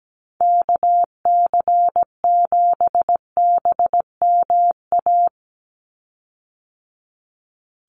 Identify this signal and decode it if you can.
Morse code, heard as a keyed tone, about 17 wpm, 700 Hz. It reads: KC7BMA
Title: USA Call Signs 17wpm